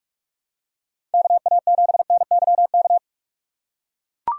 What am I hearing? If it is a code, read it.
Morse code, heard as a keyed tone, about 44 wpm, 700 Hz. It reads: KA7NYK